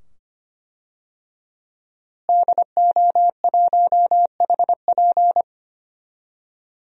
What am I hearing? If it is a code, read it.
Morse code, heard as a keyed tone, about 25 wpm, 700 Hz. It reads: DO1HP
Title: DXCC-100 Call Signs - 5 Characters - Repeated 2x - 25wpm